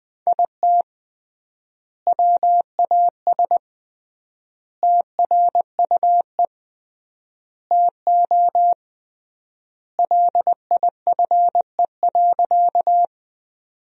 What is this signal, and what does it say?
Morse code, heard as a keyed tone, about 21 wpm, 700 Hz. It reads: IT WAS TRUE TO LIFE.
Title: 3x Word-Spacing - Sentences from Top 400 Words - 20wpm